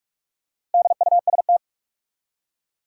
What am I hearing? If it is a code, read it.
Morse code, heard as a keyed tone, about 43 wpm, 700 Hz. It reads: DUST
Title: Four Letter Words - 45wpm